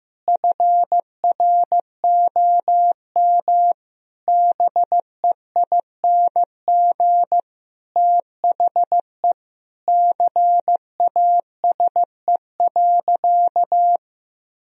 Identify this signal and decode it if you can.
Morse code, heard as a keyed tone, about 15 wpm, 700 Hz. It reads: FROM BEING THE CASE.
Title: Sentences from Top 1000 Words - Review Part 3 - 15wpm